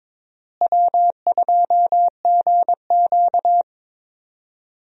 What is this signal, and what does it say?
Morse code, heard as a keyed tone, about 22 wpm, 700 Hz. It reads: W2GQ
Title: USA Call Signs - 4 Characters - Repeated 3x - 22wpm